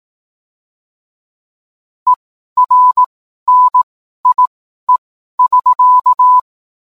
Morse code, heard as a keyed tone, about 18 wpm, spaced as for 12 wpm, 1000 Hz. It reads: ERNIE<SK>